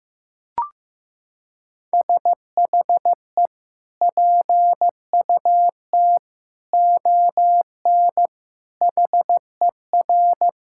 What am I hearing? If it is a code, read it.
Morse code, heard as a keyed tone, about 15 wpm, 700 Hz. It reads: SHE PUT ON HER
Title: Sentences from Top 1600 Words 15wpm